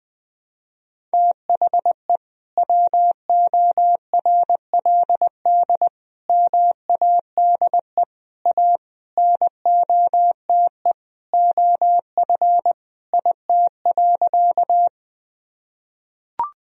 Morse code, heard as a keyed tone, about 20 wpm, 700 Hz. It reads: THE WORLD MADE A NOTE OF IT.
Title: Sentences from Top 1000 Words - Review Part 2 - 20wpm